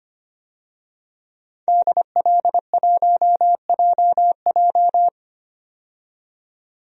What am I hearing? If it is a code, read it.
Morse code, heard as a keyed tone, about 25 wpm, 700 Hz. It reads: DL1JJ
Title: DXCC-100 Call Signs - 5 Characters - 25wpm